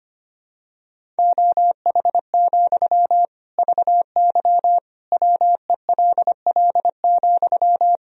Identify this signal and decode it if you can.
Morse code, heard as a keyed tone, about 25 wpm, 700 Hz. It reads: OH, VY WELL,
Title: Sentences from Top 500 Words and Abbreviations 25wpm